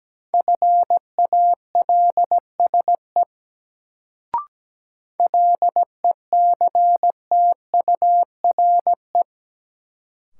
Morse code, heard as a keyed tone, about 17 wpm, 700 Hz. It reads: FALSE LECTURE
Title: Sentences from Top 1600 Words - New Word Drill - 17wpm